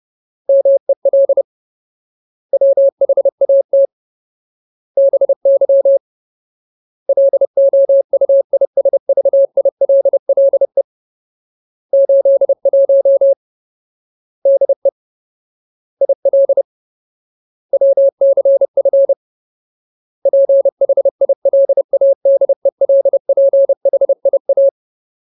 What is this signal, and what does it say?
Morse code, heard as a keyed tone, about 30 wpm, 550 Hz. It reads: MEL WHAT BY LOUISVILLE 81 DE IL WCF PHILADELPHIA
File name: Ultimate Word List / 30WPM Ultimate